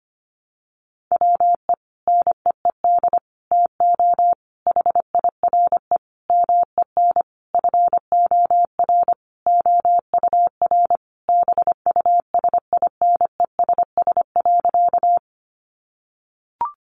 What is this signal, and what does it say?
Morse code, heard as a keyed tone, about 25 wpm, 700 Hz. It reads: WE NEED TO HIRE MEN FOR OUR BUSINESS.
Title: Sentences from Top 1400 Words 25wpm